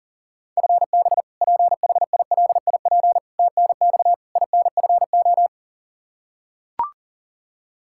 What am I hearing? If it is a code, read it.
Morse code, heard as a keyed tone, about 40 wpm, 700 Hz. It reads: FB PHILIP TNX INFO
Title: Practice QSO - Exchange 3 - More Info - 40wpm